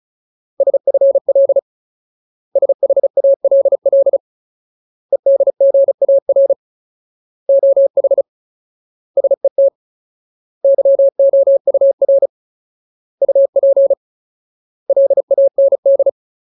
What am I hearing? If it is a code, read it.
Morse code, heard as a keyed tone, about 34 wpm, 550 Hz. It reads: SFL SHALL EDGAR OH SET YOUR UP LAND
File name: Ultimate Word List / 35WPM Ultimate